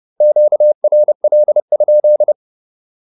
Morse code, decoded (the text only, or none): QRL?